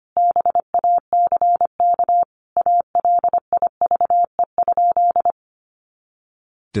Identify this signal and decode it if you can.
Morse code, heard as a keyed tone, about 25 wpm, 700 Hz. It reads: BACK ALIVE?
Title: A Sound Of Thunder 25wpm